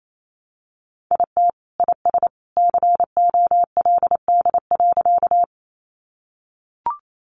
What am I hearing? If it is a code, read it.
Morse code, heard as a keyed tone, about 28 wpm, 700 Hz. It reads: IT IS COLD.
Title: Sentences from Top 400 Words 28wpm